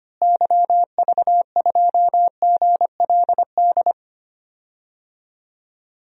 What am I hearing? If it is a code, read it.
Morse code, heard as a keyed tone, about 25 wpm, 700 Hz. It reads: YV2GLD
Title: DXCC-100 Call Signs - 6 Characters - Repeated 2x - 25wpm